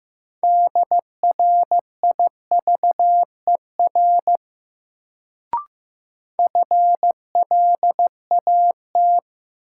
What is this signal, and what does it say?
Morse code, heard as a keyed tone, about 15 wpm, 700 Hz. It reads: DRIVER FLAT
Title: Sentences from Top 1000 Words - New Word Drill - 15wpm